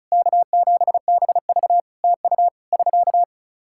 Morse code, heard as a keyed tone, about 36 wpm, 700 Hz. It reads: K7BV TU <SK>
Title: Practice QSO - Exchange 4 - Ending - 35wpm